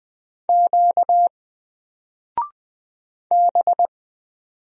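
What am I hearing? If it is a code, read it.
Morse code, heard as a keyed tone, about 20 wpm, 700 Hz. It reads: Q B
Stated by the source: Single letter and number Morse drill at 20 wpm, from N4LCD.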